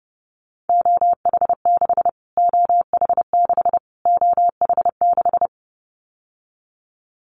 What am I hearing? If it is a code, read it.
Morse code, heard as a keyed tone, about 30 wpm, 700 Hz. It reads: OH6 OH6 OH6
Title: Callsign Prefixes for Top 25 DXCC - Repeated 3x - 30wpm